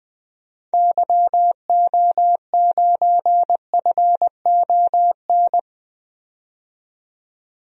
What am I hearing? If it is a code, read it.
Morse code, heard as a keyed tone, about 20 wpm, 700 Hz. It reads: YO9FON
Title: DXCC-100 Call Signs - 6 Characters - 20wpm